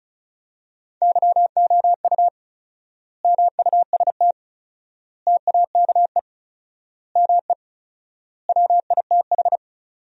Morse code, heard as a keyed tone, about 34 wpm, 700 Hz. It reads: YOU MUST TAKE ME WITH
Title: 4x Word-Spacing - Sentences from Top 200 Words - 35wpm